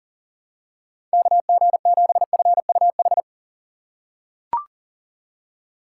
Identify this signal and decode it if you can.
Morse code, heard as a keyed tone, about 37 wpm, 700 Hz. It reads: KG7FUH
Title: USA Call Signs - 6 Characters - 2x Character Spacing - 40wpm